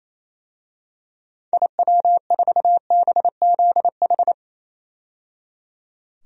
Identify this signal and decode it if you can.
Morse code, heard as a keyed tone, about 28 wpm, 700 Hz. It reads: IW4BZH
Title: DXCC-100 Call Signs - 6 Characters - 28wpm